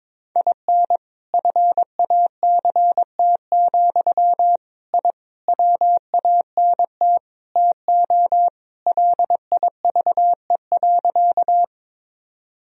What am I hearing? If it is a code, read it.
Morse code, heard as a keyed tone, about 22 wpm, 700 Hz. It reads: IN FACT, I WANT TO LIVE.
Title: Sentences from Top 1000 Words - Review Part 3 - 22wpm